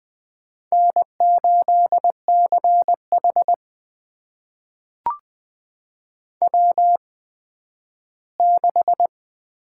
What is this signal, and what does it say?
Morse code, heard as a keyed tone, about 20 wpm, spaced as for 12 wpm, 700 Hz. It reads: N8CH W6
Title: USA Call Signs - 4 Characters - 8x Character Spacing - 20wpm